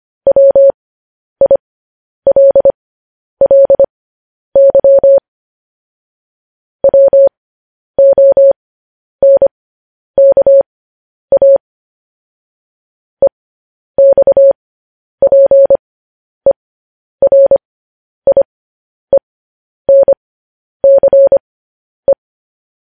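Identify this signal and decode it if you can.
Morse code, heard as a keyed tone, about 25 wpm, spaced as for 10 wpm, 550 Hz. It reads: WILLY WONKA EXPERIENCE
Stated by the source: Morse code transcription of news headlines sent at 10 wpm.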